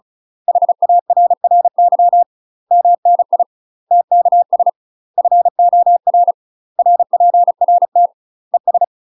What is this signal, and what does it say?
Morse code, heard as a keyed tone, about 36 wpm, 700 Hz. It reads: HARRY MNI TKS FOR RPRT ES
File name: UK MASTER QSOs TRAINING / 35WPM / UK MASTER QSOs 0009